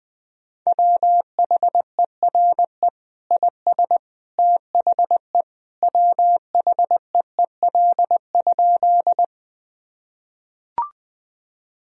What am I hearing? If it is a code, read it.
Morse code, heard as a keyed tone, about 21 wpm, 700 Hz. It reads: WHERE IS THE WHEEL?
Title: Sentences from Top 1000 Words - Review Part 3 - 20wpm